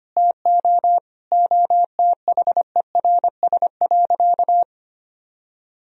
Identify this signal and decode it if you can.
Morse code, heard as a keyed tone, about 25 wpm, 700 Hz. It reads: TO OTHERS.
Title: Sentences from Top 1000 Words - Review Part 4 - 25wpm